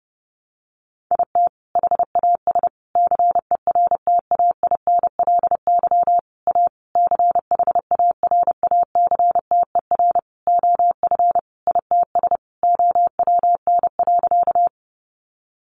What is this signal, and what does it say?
Morse code, heard as a keyed tone, about 28 wpm, 700 Hz. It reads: IT HAS CERTAINLY A CHARACTER OF ITS OWN.
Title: Sentences from Top 1200 Words 30wpm